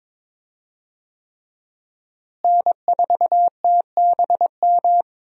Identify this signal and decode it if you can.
Morse code, heard as a keyed tone, about 22 wpm, 700 Hz. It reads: N4TBM
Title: USA Call Signs - 5 Characters - Repeated 3x - 22wpm